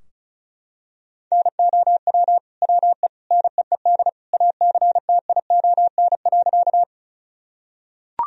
Morse code, heard as a keyed tone, about 36 wpm, 700 Hz. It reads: NOW WE NEED ACTION.
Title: Sentences from Top 1000 Words - Review Part 4 - 35wpm